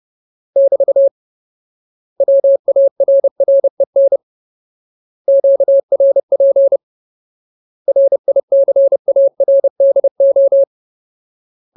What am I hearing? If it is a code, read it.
Morse code, heard as a keyed tone, about 30 wpm, 550 Hz. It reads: <BT> WARREN QRP RICARDO
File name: Ultimate Word List / 30WPM Ultimate